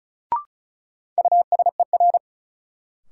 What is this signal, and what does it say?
Morse code, heard as a keyed tone, about 35 wpm, 700 Hz. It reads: USER
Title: Sentences from Top 1000 Words - Review Part 5 - New Word Drill - 35wpm